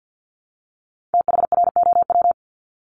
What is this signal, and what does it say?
Morse code, heard as a keyed tone, about 50 wpm, 700 Hz. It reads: THROW
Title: Top 1501-1750 Words 50wpm